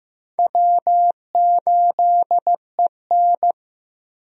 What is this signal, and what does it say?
Morse code, heard as a keyed tone, about 15 wpm, 700 Hz. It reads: W8EN
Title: USA Call Signs - 4 Characters - Repeated 3x - 15wpm